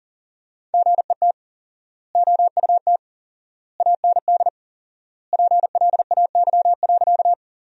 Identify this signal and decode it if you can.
Morse code, heard as a keyed tone, about 37 wpm, 700 Hz. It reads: GET OUT AND PLAY.
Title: 4x Word-Spacing - Sentences from Top 200 Words - 40wpm